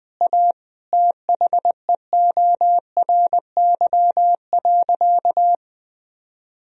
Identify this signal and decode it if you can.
Morse code, heard as a keyed tone, about 21 wpm, 700 Hz. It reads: A THEORY.
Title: Sentences from Top 1000 Words - Review Part 3 - 20wpm